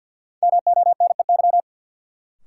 Morse code, heard as a keyed tone, about 53 wpm, 700 Hz. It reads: MONEY
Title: Sets of 4 Words - New Word Drill - 50wpm